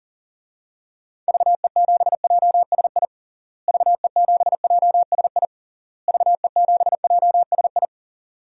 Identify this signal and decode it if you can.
Morse code, heard as a keyed tone, about 40 wpm, 700 Hz. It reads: VE7JSI VE7JSI VE7JSI
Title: DXCC-100 Call Signs - 6 Characters - Repeated 3x - 40wpm